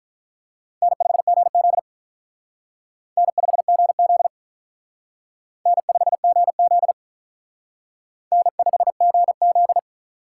Morse code, heard as a keyed tone, about 44 wpm, 700 Hz. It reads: N5GZ N5GZ N5GZ N5GZ